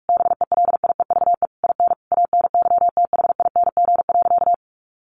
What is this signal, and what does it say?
Morse code, heard as a keyed tone, about 44 wpm, 700 Hz. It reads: BELIEVE IN ANYTHING.
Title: Sentences from Top 1500 Words 45wpm